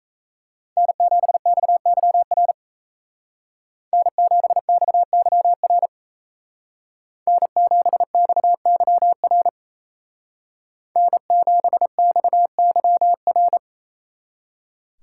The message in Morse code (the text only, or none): N7XYR N7XYR N7XYR N7XYR